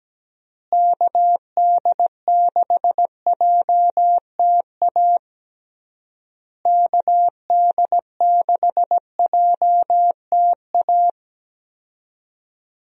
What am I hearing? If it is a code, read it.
Morse code, heard as a keyed tone, about 17 wpm, 700 Hz. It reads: KD6JTA KD6JTA
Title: USA Call Signs - 6 Characters - Repeated 3x - 17wpm